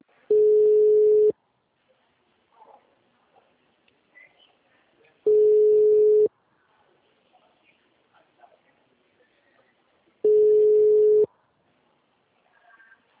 Ringback tone (Europe, 425 Hz).
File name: out- -2002- - - .71666